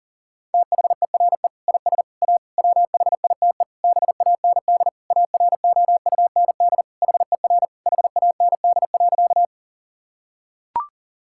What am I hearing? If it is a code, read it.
Morse code, heard as a keyed tone, about 40 wpm, 700 Hz. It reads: THERE IS A WHITE BAND AROUND HER HAND.
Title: Sentences from Top 1600 Words 40wpm